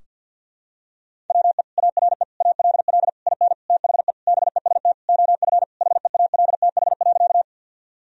Morse code, heard as a keyed tone, about 53 wpm, 700 Hz. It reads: WE ARE ALL IN THE BEST OF HEALTH.
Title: Sentences from Top 1000 Words - Review Part 3 - 50wpm